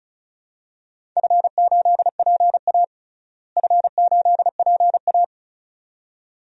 Morse code, heard as a keyed tone, about 34 wpm, 700 Hz. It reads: F8PA F8PA
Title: DXCC-100 Call Signs - 4 Characters - Repeated 2x - 35wpm